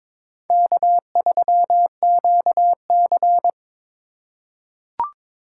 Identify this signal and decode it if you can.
Morse code, heard as a keyed tone, about 22 wpm, 700 Hz. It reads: K3QC
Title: USA Call Signs - 4 Characters - 4x Character Spacing - 22wpm